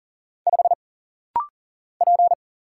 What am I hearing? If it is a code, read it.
Morse code, heard as a keyed tone, about 39 wpm, 700 Hz. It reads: 5 P
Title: Single Letter-Number - Rapid-Fire 40wpm